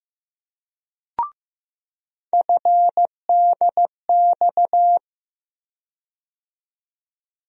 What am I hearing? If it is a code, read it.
Morse code, heard as a keyed tone, about 16 wpm, 700 Hz. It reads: FDX
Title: USA Call Sign Postfixes - 3 Letters 15wpm